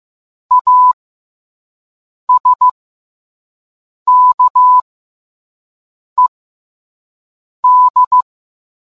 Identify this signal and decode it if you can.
Morse code, heard as a keyed tone, about 15 wpm, spaced as for 5 wpm, 1000 Hz. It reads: ASKED